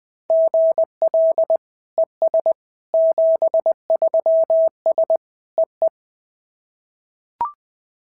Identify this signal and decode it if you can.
Morse code, heard as a keyed tone, about 20 wpm, 650 Hz. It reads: GL ES 73S EE
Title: SOTA QSO Practice 20wpm